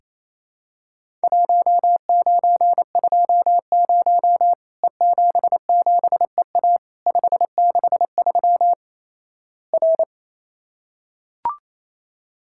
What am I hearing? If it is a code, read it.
Morse code, heard as a keyed tone, about 28 wpm, 700 Hz. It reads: 1920 E77EA 563 R
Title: Worked All Europe DX Contest - QTC Exchange - 28wpm